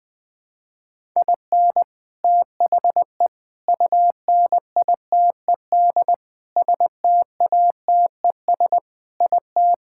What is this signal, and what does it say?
Morse code, heard as a keyed tone, about 20 wpm, 700 Hz. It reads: IN THE UNITED STATES IT
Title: Sentences with 3-Word Phrases - Spoken Beginning - 20wpm